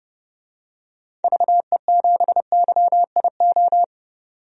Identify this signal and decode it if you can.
Morse code, heard as a keyed tone, about 30 wpm, 700 Hz. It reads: VE7YIO
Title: DXCC-100 Call Signs - 6 Characters - 30wpm